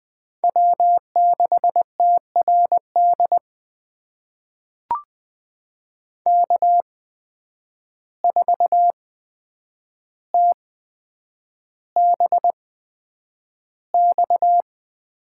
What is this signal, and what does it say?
Morse code, heard as a keyed tone, about 20 wpm, spaced as for 9 wpm, 700 Hz. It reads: W6TRD K4TBX
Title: USA Call Signs - 5 Characters - 8x Character Spacing - 20wpm